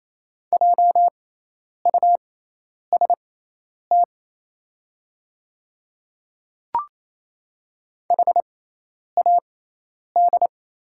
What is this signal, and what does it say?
Morse code, heard as a keyed tone, about 28 wpm, spaced as for 10 wpm, 700 Hz. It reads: JUST HAD
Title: 6x Character Spacing - Top 500 Words - 28wpm